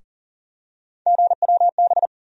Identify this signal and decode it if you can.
Morse code, heard as a keyed tone, about 40 wpm, 700 Hz. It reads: GWB
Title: USA Call Sign Postfixes - 3 Letters 40wpm